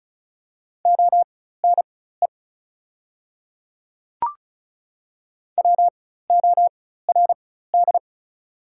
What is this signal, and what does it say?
Morse code, heard as a keyed tone, about 34 wpm, spaced as for 16 wpm, 690 Hz. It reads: ONE WORD
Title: 4x Character Spacing - Top 100 Words - 35wpm